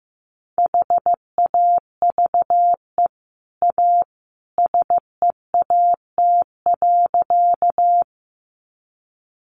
Morse code, heard as a keyed tone, about 15 wpm, 700 Hz. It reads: HAVE A SEAT.
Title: Sentences from Top 1500 Words 15wpm